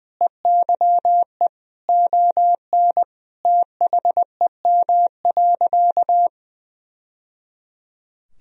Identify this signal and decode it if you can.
Morse code, heard as a keyed tone, about 20 wpm, 700 Hz. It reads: EYE ON THEM.
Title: Sentences from Top 1000 Words - Review Part 1 - 20wpm